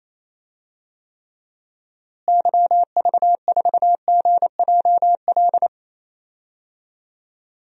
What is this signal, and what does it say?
Morse code, heard as a keyed tone, about 28 wpm, 700 Hz. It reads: YV4GJL
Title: DXCC-100 Call Signs - 6 Characters - Repeated 2x - 28wpm